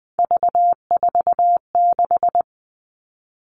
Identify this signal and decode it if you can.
Morse code, heard as a keyed tone, about 20 wpm, 700 Hz. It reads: V46
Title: All Cut Numbers - 3-Digit 20wpm